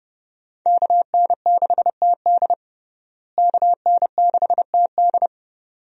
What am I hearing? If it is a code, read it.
Morse code, heard as a keyed tone, about 30 wpm, 700 Hz. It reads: KN6TD KN6TD
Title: USA Call Signs - 5 Characters - Repeated 2x - 30wpm